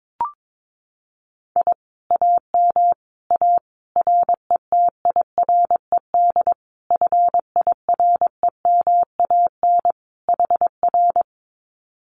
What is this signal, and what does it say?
Morse code, heard as a keyed tone, about 22 wpm, 700 Hz. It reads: I AM A RETIRED FIREMAN HR
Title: Common QSO Phrases Encore 22wpm